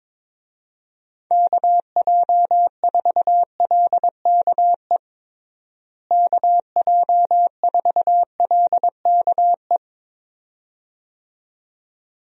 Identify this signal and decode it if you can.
Morse code, heard as a keyed tone, about 22 wpm, 700 Hz. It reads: KJ4LKE KJ4LKE
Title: USA Call Signs - 6 Characters - Repeated 2x - 22wpm